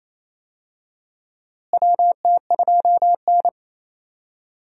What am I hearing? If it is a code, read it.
Morse code, heard as a keyed tone, about 28 wpm, 700 Hz. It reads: WT2N
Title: USA Call Signs - 4 Characters - Repeated 2x - 28wpm